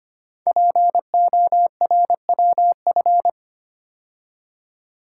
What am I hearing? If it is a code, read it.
Morse code, heard as a keyed tone, about 25 wpm, 700 Hz. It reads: PORWF
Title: Code Groups - 5 Letters - 25wpm